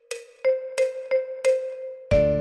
There is no code.